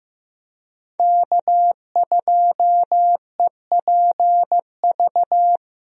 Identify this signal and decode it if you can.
Morse code, heard as a keyed tone, about 15 wpm, 700 Hz. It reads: K2EPV
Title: USA Call Signs - 5 Characters - 15wpm